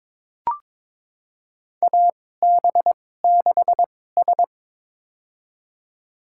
Morse code, heard as a keyed tone, about 22 wpm, spaced as for 16 wpm, 700 Hz. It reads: AB6S
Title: USA Call Signs - 2x Character Spacing - 22wpm